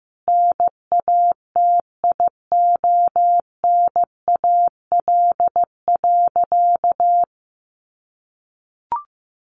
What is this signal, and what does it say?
Morse code, heard as a keyed tone, about 14 wpm, 700 Hz. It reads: NATIONAL.
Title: Sentences from Top 1200 Words 15wpm